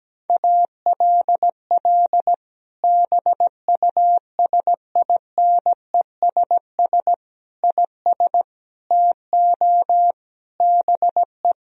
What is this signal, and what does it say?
Morse code, heard as a keyed tone, about 17 wpm, 700 Hz. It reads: ALL BUSINESS IS TO BE
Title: Sentences from Top 1000 Words - Review Part 3 - 17wpm